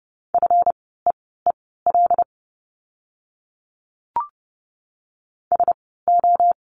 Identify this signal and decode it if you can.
Morse code, heard as a keyed tone, about 30 wpm, spaced as for 17 wpm, 700 Hz. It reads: FEEL SO
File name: course / 30:10wpm / 230 Lesson 37f - Letter x - Words 30wpm